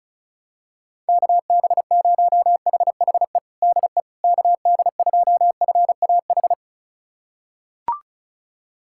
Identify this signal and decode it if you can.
Morse code, heard as a keyed tone, about 34 wpm, 700 Hz. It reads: KB0HHE DE KD2FAH
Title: USA Callsign DE Callsign 35wpm